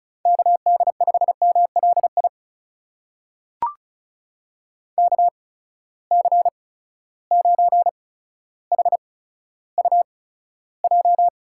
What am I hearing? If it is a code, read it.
Morse code, heard as a keyed tone, about 34 wpm, spaced as for 15 wpm, 700 Hz. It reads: KD5MLI KC9HUJ
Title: USA Call Signs - 6 Characters - 8x Character Spacing - 35wpm